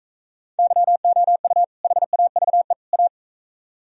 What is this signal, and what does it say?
Morse code, heard as a keyed tone, about 40 wpm, 700 Hz. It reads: YOU HAVE A